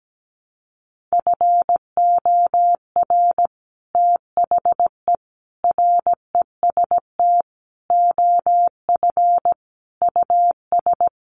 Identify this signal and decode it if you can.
Morse code, heard as a keyed tone, about 17 wpm, 700 Hz. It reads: FOR THE REST OF US
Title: Sets of 5 Words Encore 17wpm